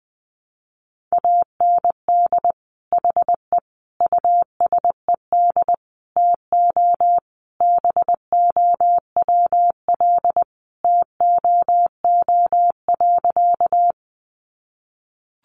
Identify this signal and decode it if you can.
Morse code, heard as a keyed tone, about 19 wpm, 700 Hz. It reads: AND HE USED TO BOWL TOO.
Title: Sentences from Top 1200 Words 20wpm